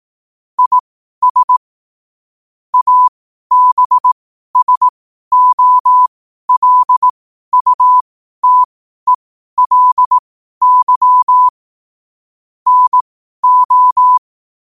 Morse code, heard as a keyed tone, about 18 wpm, spaced as for 12 wpm, 1000 Hz. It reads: IS ABSOLUTELY NO
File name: qotdfiles / 12wpm